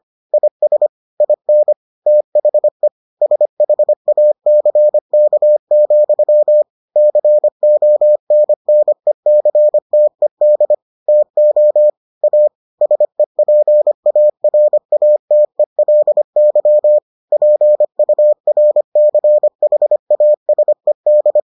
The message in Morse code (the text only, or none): IS IN THE SHACK, CONNECTED TO A SEPARATELY PURCHASED